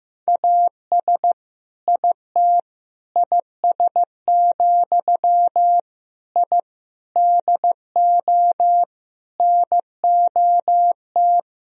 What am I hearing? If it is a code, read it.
Morse code, heard as a keyed tone, about 15 wpm, 700 Hz. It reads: AS IT IS, I DO NOT
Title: Sentences from Top 1000 Words - Review Part 2 - 15wpm